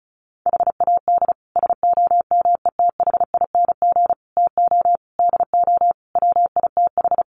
Morse code, heard as a keyed tone, about 34 wpm, 700 Hz. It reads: HAD SOMETHING TO DO WITH
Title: Sets of 5 Words 35wpm